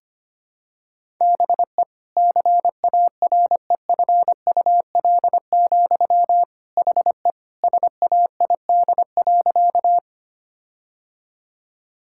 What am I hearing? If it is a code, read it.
Morse code, heard as a keyed tone, about 25 wpm, 700 Hz. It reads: BE CAREFUL, HE SAID.
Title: Sentences from Top 1600 Words 25wpm